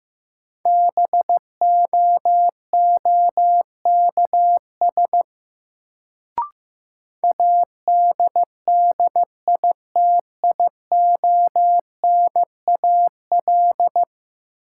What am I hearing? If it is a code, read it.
Morse code, heard as a keyed tone, about 15 wpm, 700 Hz. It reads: BOOKS ADDITIONAL
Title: Sentences from Top 1000 Words - Review Part 4 - New Word Drill - 15wpm